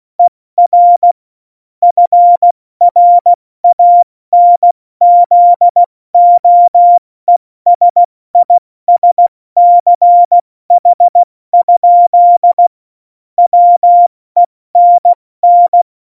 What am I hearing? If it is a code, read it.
Morse code, heard as a keyed tone, about 16 wpm, 700 Hz. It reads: ER FRANZOESISCH? WENN
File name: Morsekurs / Witze / Witz 20. November 2018, 70 F80